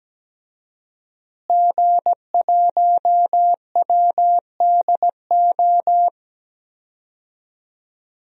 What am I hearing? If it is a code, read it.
Morse code, heard as a keyed tone, about 17 wpm, 700 Hz. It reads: G1WDO
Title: DXCC-100 Call Signs - 5 Characters - Repeated 2x - 17wpm